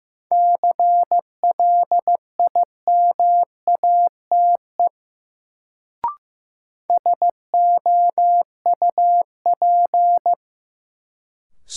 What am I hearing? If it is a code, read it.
Morse code, heard as a keyed tone, about 15 wpm, 700 Hz. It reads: CLIMATE SOUP
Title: Sentences from Top 1600 Words - New Word Drill Alt - 15wpm